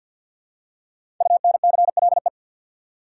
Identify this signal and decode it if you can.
Morse code, heard as a keyed tone, about 52 wpm, 700 Hz. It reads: UNCLE
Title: Five Letter Words - 50wpm